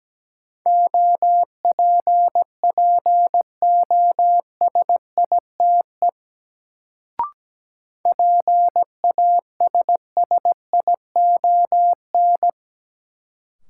Morse code, heard as a keyed tone, about 17 wpm, 700 Hz. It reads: OPPOSITE PASSION
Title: Sentences from Top 1300 Words - New Word Drill - 17wpm